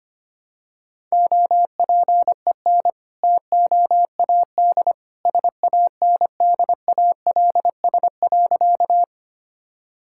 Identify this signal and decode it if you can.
Morse code, heard as a keyed tone, about 25 wpm, 700 Hz. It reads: OPEN TOAD SANDALS.
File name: practice / 25wpm / All-Age Appropriate Jokes - 25wpm